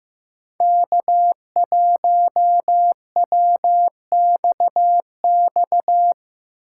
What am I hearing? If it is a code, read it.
Morse code, heard as a keyed tone, about 15 wpm, 700 Hz. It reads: K1WXX
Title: USA Call Signs - 4x Character Spacing - 15wpm